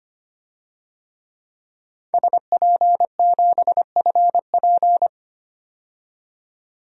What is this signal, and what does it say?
Morse code, heard as a keyed tone, about 25 wpm, 700 Hz. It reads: SP7FP
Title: DXCC-100 Call Signs - 5 Characters - 25wpm